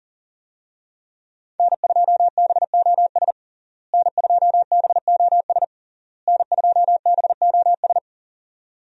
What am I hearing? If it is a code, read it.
Morse code, heard as a keyed tone, about 40 wpm, 700 Hz. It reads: N2BOS N2BOS N2BOS
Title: USA Call Signs - 5 Characters - Repeated 3x - 40wpm